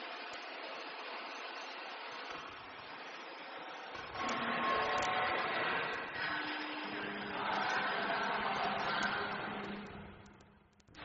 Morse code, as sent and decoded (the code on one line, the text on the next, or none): none